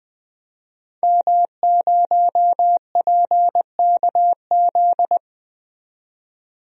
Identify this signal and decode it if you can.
Morse code, heard as a keyed tone, about 21 wpm, 700 Hz. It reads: M0PKZ
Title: DXCC-100 Call Signs - 5 Characters - 20wpm